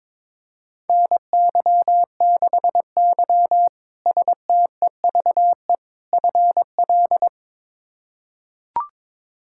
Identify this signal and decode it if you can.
Morse code, heard as a keyed tone, about 22 wpm, 700 Hz. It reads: NY6Y STEVE FL
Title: K1USN SST QSO Practice - 22wpm